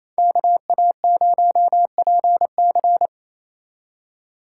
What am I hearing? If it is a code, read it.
Morse code, heard as a keyed tone, about 38 wpm, 700 Hz. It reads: KA0PC